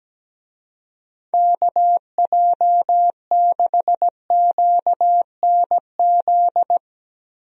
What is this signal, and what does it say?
Morse code, heard as a keyed tone, about 17 wpm, 700 Hz. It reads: KJ6QNZ
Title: USA Call Signs - 6 Characters - Repeated 2x - 17wpm